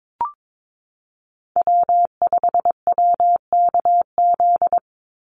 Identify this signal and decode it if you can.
Morse code, heard as a keyed tone, about 22 wpm, 700 Hz. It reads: W5WKZ
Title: USA Call Signs 22wpm